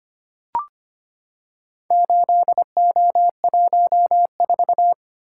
Morse code, heard as a keyed tone, about 25 wpm, 700 Hz. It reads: 8O14
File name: practice / 25wpm / Four Letter-Number Combinations - 25wpm